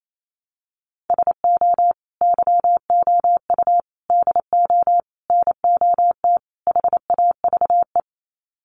Morse code, heard as a keyed tone, about 28 wpm, 700 Hz. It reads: SO YOU DO NOT HAVE
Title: Sets of 5 Words Encore 28wpm